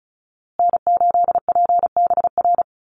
Morse code, heard as a keyed tone, about 34 wpm, 700 Hz. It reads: N8PBR
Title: USA Call Signs Repeated 2x 35wpm